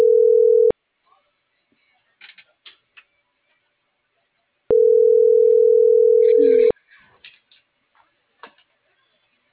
Ringback tone (North America), 440 + 480 Hz.